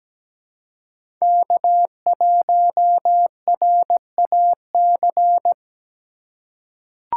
Morse code, heard as a keyed tone, about 17 wpm, 700 Hz. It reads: K1RAC
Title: USA Call Signs - 5 Characters - 4x Character Spacing - 17wpm